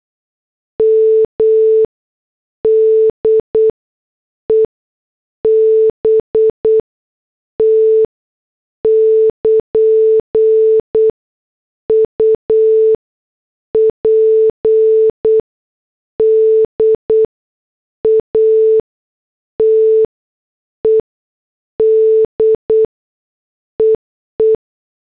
Morse code, heard as a keyed tone, about 8 wpm, spaced as for 6 wpm, 440 Hz. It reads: M DEBT <KN> UPDATED EE